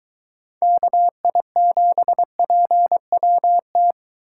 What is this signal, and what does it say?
Morse code, heard as a keyed tone, about 23 wpm, 700 Hz. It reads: KI7PWT